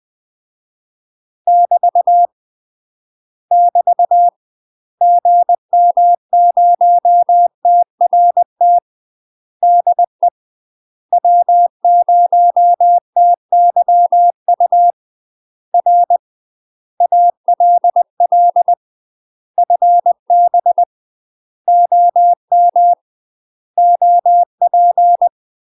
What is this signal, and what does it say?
Morse code, heard as a keyed tone, about 20 wpm, 700 Hz. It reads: <BT> <BT> GM0TRT DE W0TYU R ALL FB OM OP
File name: UK MASTER QSOs TRAINING / 20WPM / UK MASTER QSOs 0050